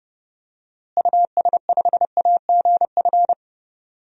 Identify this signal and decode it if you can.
Morse code, heard as a keyed tone, about 32 wpm, 700 Hz. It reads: US5AGF
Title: DXCC-100 Call Signs - 6 Characters - Repeated 2x - 30wpm